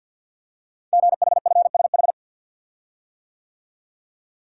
Morse code, heard as a keyed tone, about 52 wpm, 700 Hz. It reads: GHFSH
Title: Code Groups - 5 Letters - 50wpm